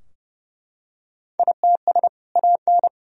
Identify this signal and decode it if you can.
Morse code, heard as a keyed tone, about 32 wpm, 700 Hz. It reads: ITS AN
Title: Sentences from Top 1000 Words - Review Part 4 - 30wpm